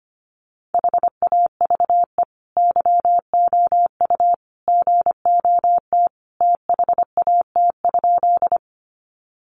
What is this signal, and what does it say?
Morse code, heard as a keyed tone, about 25 wpm, 700 Hz. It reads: HAVE YOU GOT THAT?
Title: Sentences from Top 300 Words 25wpm